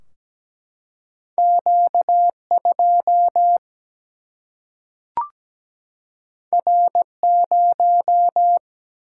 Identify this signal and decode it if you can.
Morse code, heard as a keyed tone, about 17 wpm, 700 Hz. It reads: Q 2 R 0
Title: Single-Letter Single-Number 17wpm